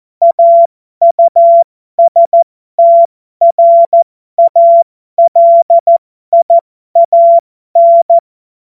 Morse code, heard as a keyed tone, about 14 wpm, spaced as for 12 wpm, 680 Hz. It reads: AUSTRALIAN